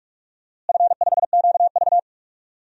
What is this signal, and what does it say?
Morse code, heard as a keyed tone, about 45 wpm, 700 Hz. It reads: F5QV
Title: DXCC-100 Call Signs - 4 Characters - Repeated 3x - 45wpm